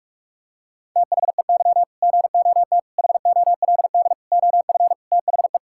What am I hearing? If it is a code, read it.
Morse code, heard as a keyed tone, about 46 wpm, 700 Hz. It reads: THEY GOT HOLD OF THE
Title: Sentences from Top 1600 Words 45wpm